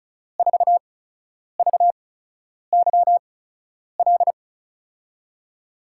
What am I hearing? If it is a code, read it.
Morse code, heard as a keyed tone, about 36 wpm, 700 Hz. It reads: 4 V Y L
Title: USA Call Signs - 6 Characters - 8x Character Spacing - 35wpm